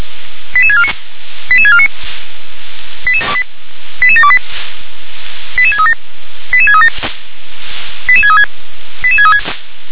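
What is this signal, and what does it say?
Five-tone selective call (ZVEI), heard as quick runs of five tones, tones of about 70 ms. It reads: 88649 88649 88728 88637 88637 88636 88636
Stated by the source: Five-tone ZVEI selective call received by SDR.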